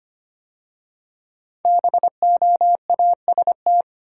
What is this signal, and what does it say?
Morse code, heard as a keyed tone, about 25 wpm, 700 Hz. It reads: BOAST
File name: 5 Letter Words - First 3 Letters Given - 25wpm